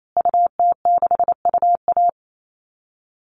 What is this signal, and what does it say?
Morse code, heard as a keyed tone, about 28 wpm, 700 Hz. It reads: UT6UA
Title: Call Signs Repeated 2x - 28wpm